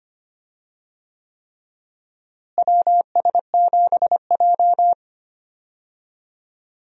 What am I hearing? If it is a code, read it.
Morse code, heard as a keyed tone, about 25 wpm, 700 Hz. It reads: WS7J
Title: USA Call Signs - 4 Characters - Repeated 3x - 25wpm